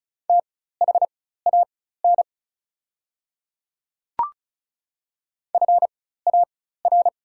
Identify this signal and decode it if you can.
Morse code, heard as a keyed tone, about 36 wpm, spaced as for 16 wpm, 700 Hz. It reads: THAN FAR
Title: 4x Character Spacing - Top 300 Words - 35wpm